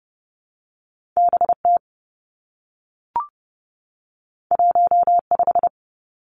Morse code, heard as a keyed tone, about 30 wpm, 700 Hz. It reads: B T 1 5
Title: Two Challenging Letter-Number Combinations 30wpm